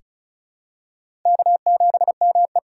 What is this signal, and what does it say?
Morse code, heard as a keyed tone, about 34 wpm, 690 Hz. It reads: K7ME
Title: USA Call Signs - 3x Character Spacing - 35wpm